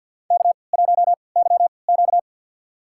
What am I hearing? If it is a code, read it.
Morse code, heard as a keyed tone, about 52 wpm, spaced as for 29 wpm, 700 Hz. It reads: K1YQ
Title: USA Call Signs - 4 Characters - 3x Character Spacing - 50wpm